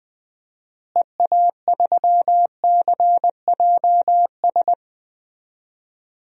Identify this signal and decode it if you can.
Morse code, heard as a keyed tone, about 21 wpm, 700 Hz. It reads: EA3CJS
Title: DXCC-100 Call Signs - 6 Characters - Repeated 2x - 20wpm